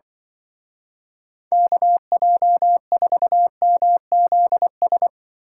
Morse code, heard as a keyed tone, about 25 wpm, 700 Hz. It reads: KJ4MZS
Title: Speed Racing - USA Call Signs 20wpm